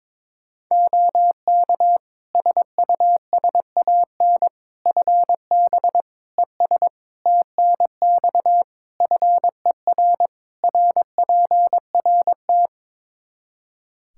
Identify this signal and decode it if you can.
Morse code, heard as a keyed tone, about 22 wpm, 700 Hz. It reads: OK SUSAN FB ES TNX FER RPRT
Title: Practice QSO - Exchange 2 - Info - 22wpm